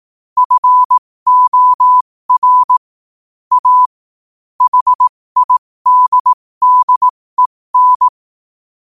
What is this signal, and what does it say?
Morse code, heard as a keyed tone, about 18 wpm, spaced as for 15 wpm, 1000 Hz. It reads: FOR A HIDDEN
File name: qotdfiles / 15wpm